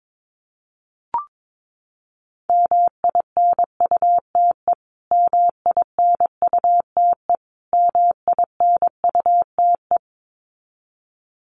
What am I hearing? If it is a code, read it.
Morse code, heard as a keyed tone, about 22 wpm, 700 Hz. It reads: MINUTE MINUTE MINUTE
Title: Top 500 Words Repeated 22wpm